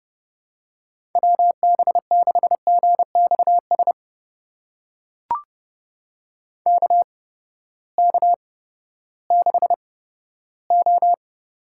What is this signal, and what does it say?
Morse code, heard as a keyed tone, about 31 wpm, spaced as for 15 wpm, 700 Hz. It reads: WB6GXS KK6O
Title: USA Call Signs - 6 Characters - 8x Character Spacing - 30wpm